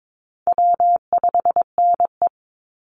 Morse code, heard as a keyed tone, about 22 wpm, 700 Hz. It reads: W5NE
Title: Call Signs - 22wpm